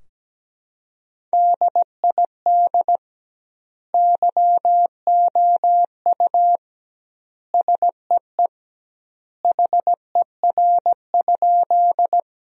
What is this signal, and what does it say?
Morse code, heard as a keyed tone, about 17 wpm, 700 Hz. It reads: DID YOU SEE HER?
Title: 2x Word-Spacing - Sentences from Top 100 Words - 17wpm